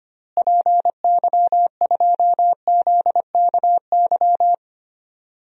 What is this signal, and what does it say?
Morse code, heard as a keyed tone, about 25 wpm, 700 Hz. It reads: PY2ZKY
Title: DXCC-100 Call Signs - 6 Characters - Repeated 3x - 25wpm